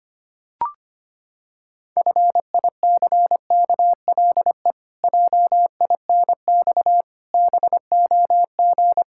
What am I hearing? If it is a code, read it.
Morse code, heard as a keyed tone, about 25 wpm, 700 Hz. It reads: FICKLE JINX BOG
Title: Pangram Challenge v2 - 25wpm